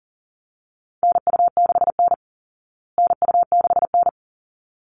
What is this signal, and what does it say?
Morse code, heard as a keyed tone, about 40 wpm, 700 Hz. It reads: NU6N NU6N
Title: Call Signs Repeated 3x - 40wpm